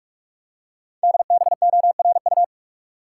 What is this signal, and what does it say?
Morse code, heard as a keyed tone, about 44 wpm, 700 Hz. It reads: DBORU
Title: Code Groups - 5 Letters - 45wpm